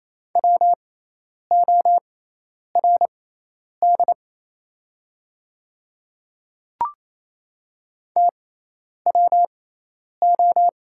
Morse code, heard as a keyed tone, about 27 wpm, spaced as for 10 wpm, 690 Hz. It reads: WORD TWO
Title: 6x Character Spacing - Top 100 Words - 28wpm